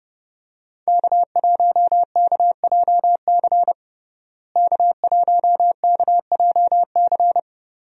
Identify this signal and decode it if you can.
Morse code, heard as a keyed tone, about 30 wpm, 700 Hz. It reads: K1KJC K1KJC
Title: USA Call Signs - 5 Characters - Repeated 3x - 30wpm